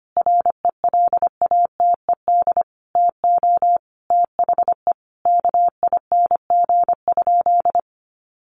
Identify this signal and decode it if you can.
Morse code, heard as a keyed tone, about 25 wpm, 700 Hz. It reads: RELATED TO THE KING?
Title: Sentences from Top 1400 Words 25wpm